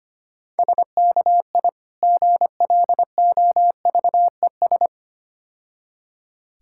Morse code, heard as a keyed tone, about 26 wpm, 700 Hz. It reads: SKI GLOVES
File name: practice / 25wpm / Difficult Words - Speed-Racing with Hints - 25wpm